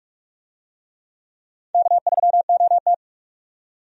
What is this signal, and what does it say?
Morse code, heard as a keyed tone, about 46 wpm, 700 Hz. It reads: K3OT
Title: USA Call Signs - 4 Characters - 45wpm